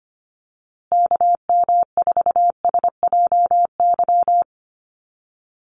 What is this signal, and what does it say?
Morse code, heard as a keyed tone, about 25 wpm, 700 Hz. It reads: KM4SJY
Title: USA Call Signs Repeated 2x 25wpm